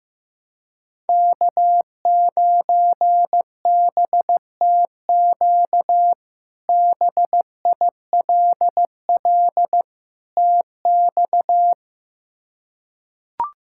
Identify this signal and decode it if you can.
Morse code, heard as a keyed tone, about 15 wpm, 700 Hz. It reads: K9BTQ BILL TX
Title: K1USN SST QSO Practice - 15wpm